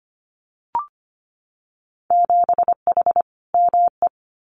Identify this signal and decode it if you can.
Morse code, heard as a keyed tone, about 25 wpm, 690 Hz. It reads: 7H ME
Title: Winter Field Day Exchanges Encore 25wpm